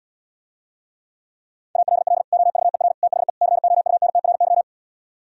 Morse code, heard as a keyed tone, about 77 wpm, 700 Hz. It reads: ALL OVER THE COUNTRY